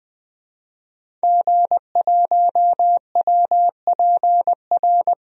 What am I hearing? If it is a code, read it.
Morse code, heard as a keyed tone, about 20 wpm, 700 Hz. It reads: G1WPR
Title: DXCC-100 Call Signs - 5 Characters - 20wpm